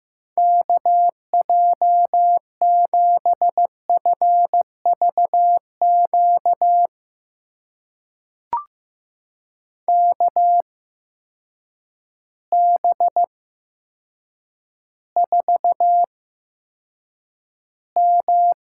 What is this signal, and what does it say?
Morse code, heard as a keyed tone, about 15 wpm, spaced as for 8 wpm, 700 Hz. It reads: KJ7FVQ KB4M
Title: USA Call Signs - 6 Characters - 8x Character Spacing - 15wpm